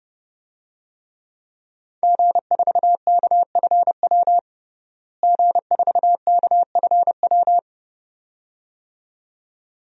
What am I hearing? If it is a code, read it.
Morse code, heard as a keyed tone, about 30 wpm, 700 Hz. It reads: G4KFW G4KFW
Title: DXCC-100 Call Signs - 5 Characters - Repeated 2x - 30wpm